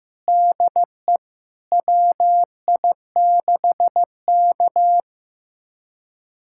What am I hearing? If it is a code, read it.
Morse code, heard as a keyed tone, about 15 wpm, 700 Hz. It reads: DE WI6K
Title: USA Callsign DE Callsign - 4 Characters - 15wpm